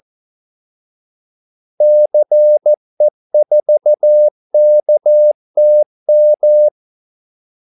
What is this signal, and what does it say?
Morse code, heard as a keyed tone, about 14 wpm, 600 Hz. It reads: CE4KTM
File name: Callsigns-14wpm